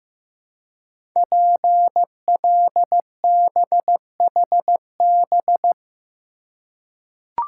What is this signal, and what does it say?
Morse code, heard as a keyed tone, about 15 wpm, 700 Hz. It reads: PLBHB
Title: Code Groups - 5 Letter:Numbers - 15wpm